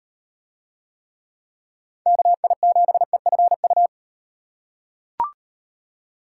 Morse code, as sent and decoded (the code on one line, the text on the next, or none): -.- .. --... ..-. ..-
KI7EFU